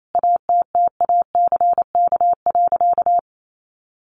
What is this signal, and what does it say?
Morse code, heard as a keyed tone, about 28 wpm, 700 Hz. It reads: ATTACK.
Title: Sentences from Top 1500 Words 28wpm